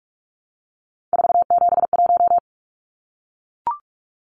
Morse code, heard as a keyed tone, about 44 wpm, 700 Hz. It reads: 471
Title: Numbers - 3-Digit 45wpm